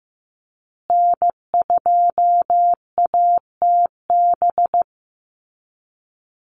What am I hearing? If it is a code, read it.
Morse code, heard as a keyed tone, about 15 wpm, 700 Hz. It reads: N2ATB
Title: Call Signs Repeated 3x - 15wpm